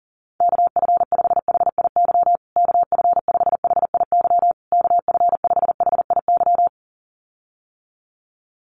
Morse code, heard as a keyed tone, about 40 wpm, 700 Hz. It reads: KF5HIY KF5HIY KF5HIY
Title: USA Call Signs Repeated 3x 40wpm